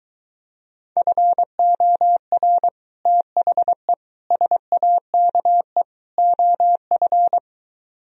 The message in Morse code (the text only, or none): FOR THE SAKE OF